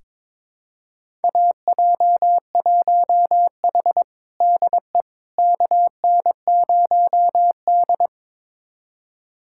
Morse code, heard as a keyed tone, about 22 wpm, 700 Hz. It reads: AJ1H DE KN0D
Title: USA Callsign DE Callsign - 4 Characters - 22wpm